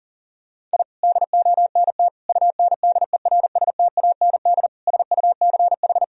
Morse code, heard as a keyed tone, about 40 wpm, 700 Hz. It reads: I DONT UNDERSTAND SUCH
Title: Sentences from Top 1000 Words - Review Part 3 - 40wpm